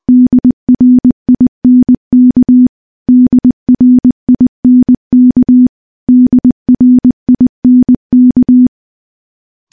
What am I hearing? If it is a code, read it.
Morse code, heard as a keyed tone, about 20 wpm, 250 Hz. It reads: DRINK DRINK DRINK
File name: morse-training-20WPM